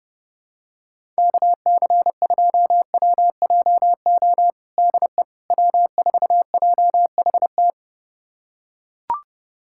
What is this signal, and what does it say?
Morse code, heard as a keyed tone, about 32 wpm, 700 Hz. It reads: KC2WJO DE W4JHT
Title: USA Callsign DE Callsign 30wpm